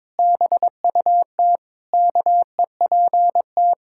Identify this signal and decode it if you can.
Morse code, heard as a keyed tone, about 22 wpm, 700 Hz. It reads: BUT KEPT
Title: Pangram Challenge v2 - 22wpm